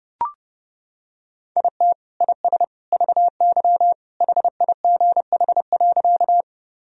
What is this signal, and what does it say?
Morse code, heard as a keyed tone, about 32 wpm, 700 Hz. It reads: IT IS VY HIGH.
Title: Sentences from Top 200 Words and Abbreviations 30wpm